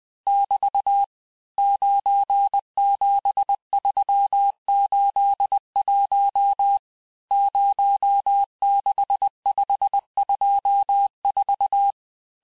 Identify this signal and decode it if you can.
Morse code, heard as a keyed tone, about 20 wpm, 800 Hz. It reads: <BT> 97381 06524
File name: cours-1584-teste-chiffre-vitesse-19mpm